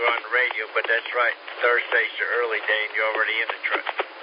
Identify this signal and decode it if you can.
Emergency Alert System (SAME) tones, heard as harsh buzzing bursts, with speech. It is heard from 0.2 to 3.8 s.